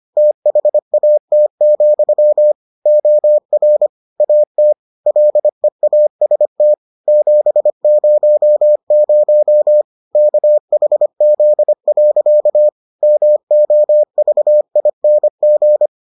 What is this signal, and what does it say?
Morse code, heard as a keyed tone, about 25 wpm, 600 Hz. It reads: THAT, OR AT LEAST 700 KHZ. MOVING